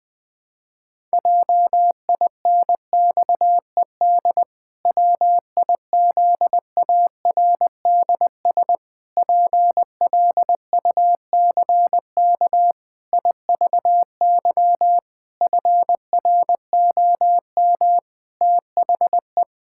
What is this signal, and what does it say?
Morse code, heard as a keyed tone, about 21 wpm, 700 Hz. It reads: JINXED WIZARDS PLUCK IVY FROM THE
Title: Pangram Challenge v2 - 20wpm